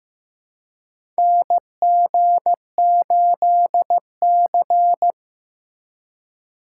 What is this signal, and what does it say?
Morse code, heard as a keyed tone, about 15 wpm, 700 Hz. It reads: NG8C
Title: USA Call Signs - 4 Characters - 4x Character Spacing - 15wpm